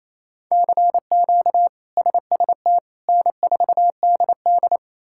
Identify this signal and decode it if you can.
Morse code, heard as a keyed tone, about 27 wpm, 700 Hz. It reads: CQ SST N4DD
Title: K1USN SST QSO Practice - 28wpm